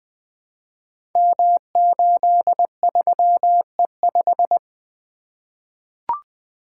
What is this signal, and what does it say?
Morse code, heard as a keyed tone, about 20 wpm, 700 Hz. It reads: M83E5
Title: Code Groups - 5 Letter:Numbers - 20wpm